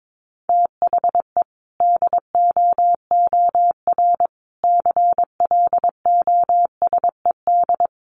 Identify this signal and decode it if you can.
Morse code, heard as a keyed tone, about 22 wpm, 700 Hz. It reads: THE DOOR CLOSED
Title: Sentences from Top 1400 Words 22wpm